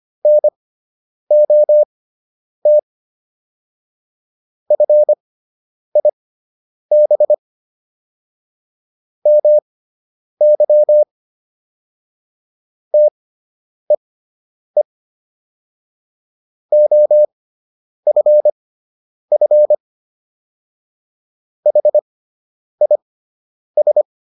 Morse code, heard as a keyed tone, about 25 wpm, spaced as for 9 wpm, 600 Hz. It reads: NOT FIB MY TEE OFF HIS